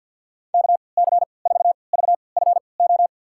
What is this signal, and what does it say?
Morse code, heard as a keyed tone, about 52 wpm, spaced as for 29 wpm, 700 Hz. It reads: KC4VFO
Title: USA Call Signs - 6 Characters - 3x Character Spacing - 50wpm